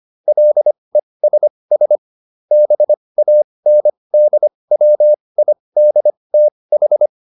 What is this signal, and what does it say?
Morse code, heard as a keyed tone, about 25 wpm, spaced as for 20 wpm, 600 Hz. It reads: LESS BANDWIDTH